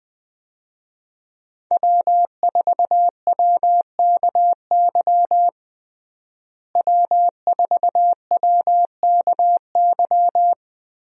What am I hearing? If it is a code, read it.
Morse code, heard as a keyed tone, about 21 wpm, 700 Hz. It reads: W4WKY W4WKY
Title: USA Call Signs - 5 Characters - Repeated 2x - 20wpm